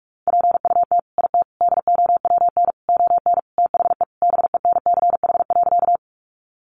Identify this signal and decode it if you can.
Morse code, heard as a keyed tone, about 44 wpm, 700 Hz. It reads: PUT IT DOWN ON THE BENCH.
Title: Sentences from Top 1400 Words 45wpm